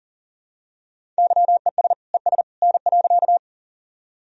Morse code, heard as a keyed tone, about 37 wpm, 700 Hz. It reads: YES ES N.
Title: Sentences from Top 500 Words and Abbreviations 40wpm